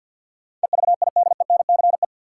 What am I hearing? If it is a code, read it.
Morse code, heard as a keyed tone, about 52 wpm, 700 Hz. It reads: EVIDENCE
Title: Sentences from Top 1300 Words - New Word Drill - 50wpm